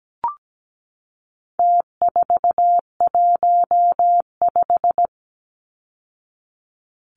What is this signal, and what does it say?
Morse code, heard as a keyed tone, about 17 wpm, 700 Hz. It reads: T415
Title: Numbers - Contest Style - 4-Digit 17wpm